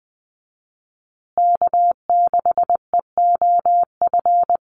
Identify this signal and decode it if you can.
Morse code, heard as a keyed tone, about 20 wpm, 700 Hz. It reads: K6EOF
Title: USA Call Signs Repeated 3x 20wpm